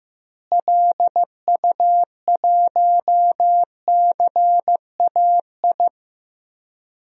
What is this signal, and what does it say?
Morse code, heard as a keyed tone, about 15 wpm, 700 Hz. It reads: LU1CAI
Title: DXCC-100 Call Signs - 6 Characters - Repeated 3x - 15wpm